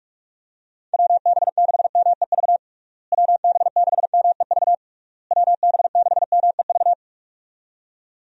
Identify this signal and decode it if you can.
Morse code, heard as a keyed tone, about 46 wpm, 700 Hz. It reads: WB6MEV WB6MEV WB6MEV
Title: USA Call Signs - 6 Characters - Repeated 3x - 45wpm